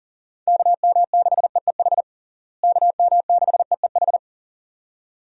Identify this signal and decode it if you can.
Morse code, heard as a keyed tone, about 37 wpm, 700 Hz. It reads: KM6EEH KM6EEH
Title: USA Call Signs - 6 Characters - Repeated 2x - 40wpm